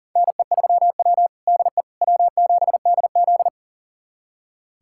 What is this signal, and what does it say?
Morse code, heard as a keyed tone, about 40 wpm, 700 Hz. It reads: NE3W DE W7DZ
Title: USA Callsign DE Callsign - 4 Characters - 40wpm